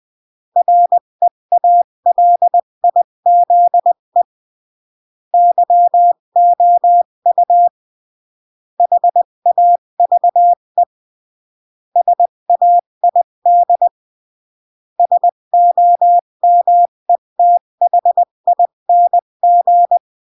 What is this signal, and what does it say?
Morse code, heard as a keyed tone, about 20 wpm, 700 Hz. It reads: REALIZE YOU HAVE SAID SOMETHING
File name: CW / jodorowsky / 82 commandments 20-15